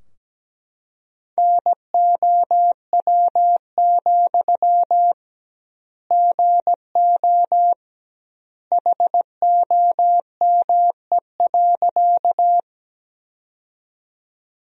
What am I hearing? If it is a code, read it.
Morse code, heard as a keyed tone, about 17 wpm, 700 Hz. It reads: NOW, GO HOME.
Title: 2x Word-Spacing - Sentences from Top 200 Words - 17wpm